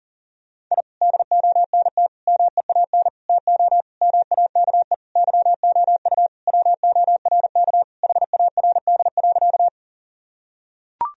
Morse code, heard as a keyed tone, about 40 wpm, 700 Hz. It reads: I DONT MEAN TO MAKE YOU WORK HARD.
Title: Sentences from Top 1000 Words - Review Part 1 - 40wpm